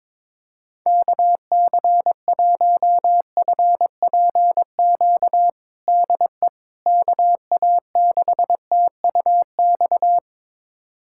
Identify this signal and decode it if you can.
Morse code, heard as a keyed tone, about 22 wpm, 700 Hz. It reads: KC1FPQ DE KA6TUX
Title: USA Callsign DE Callsign 22wpm